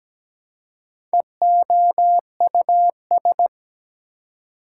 Morse code, heard as a keyed tone, about 17 wpm, 700 Hz. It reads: EOUS
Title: Prefixes and Suffixes 17wpm